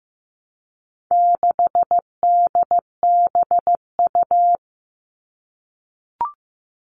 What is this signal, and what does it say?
Morse code, heard as a keyed tone, about 15 wpm, 700 Hz. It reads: 6DBU
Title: All Cut Numbers - 4-Digit 15wpm